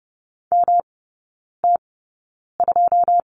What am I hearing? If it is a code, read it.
Morse code, heard as a keyed tone, about 30 wpm, spaced as for 9 wpm, 700 Hz. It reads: MT2
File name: Lesson 18e - Letter m - Three Characters 30wpm